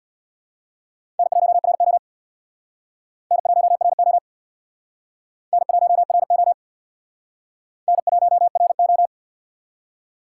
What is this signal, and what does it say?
Morse code, heard as a keyed tone, about 60 wpm, 700 Hz. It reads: N1RO N1RO N1RO N1RO
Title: Speed Racing - USA 1x2 Call Signs 50wpm